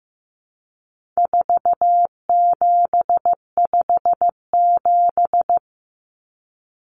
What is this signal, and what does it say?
Morse code, heard as a keyed tone, about 15 wpm, 700 Hz. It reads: 4757
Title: Numbers - 4-Digit 15wpm